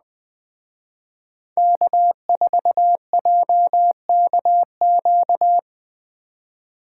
Morse code, heard as a keyed tone, about 21 wpm, 700 Hz. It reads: K4JKQ
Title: Speed Racing - USA 1x3 Call Signs 17wpm